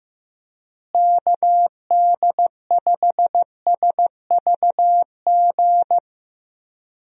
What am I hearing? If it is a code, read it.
Morse code, heard as a keyed tone, about 16 wpm, 700 Hz. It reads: KD5SVG
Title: USA Call Signs - 6 Characters - Repeated 2x - 15wpm